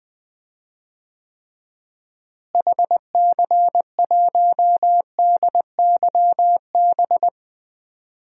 Morse code, heard as a keyed tone, about 20 wpm, 700 Hz. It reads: HC1DYB
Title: DXCC-100 Call Signs - 6 Characters - Repeated 2x - 20wpm